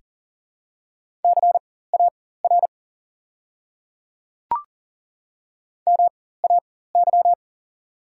Morse code, heard as a keyed tone, about 40 wpm, spaced as for 19 wpm, 700 Hz. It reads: CAR MAY
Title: 4x Character Spacing - Top 300 Words - 40wpm